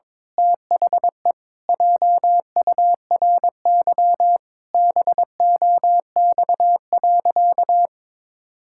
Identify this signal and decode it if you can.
Morse code, heard as a keyed tone, about 22 wpm, 700 Hz. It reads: THE JURY BOX.
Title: Pangram Challenge v2 - 22wpm